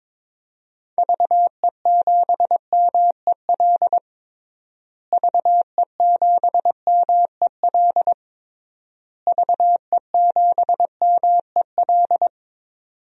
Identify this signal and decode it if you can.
Morse code, heard as a keyed tone, about 22 wpm, 700 Hz. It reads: VE7MEL VE7MEL VE7MEL
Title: DXCC-100 Call Signs - 6 Characters - Repeated 3x - 22wpm